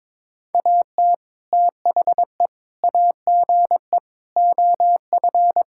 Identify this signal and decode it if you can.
Morse code, heard as a keyed tone, about 21 wpm, 700 Hz. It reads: AT THE AGE OF
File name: practice / 22wpm / Sentences with 4-Word Phrases - 22wpm